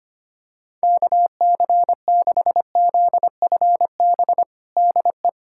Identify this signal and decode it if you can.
Morse code, heard as a keyed tone, about 25 wpm, 700 Hz. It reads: KC6ZFB DE
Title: USA Callsign DE Callsign 25wpm